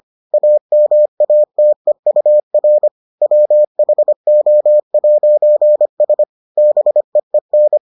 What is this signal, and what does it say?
Morse code, heard as a keyed tone, about 25 wpm, 600 Hz. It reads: AMATEUR WHO'S BEEN